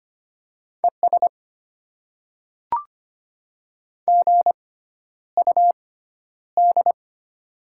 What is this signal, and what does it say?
Morse code, heard as a keyed tone, about 25 wpm, spaced as for 11 wpm, 700 Hz. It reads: ES GUD
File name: Top 20 QSO Elements - 6x Character Spacing - 25wpm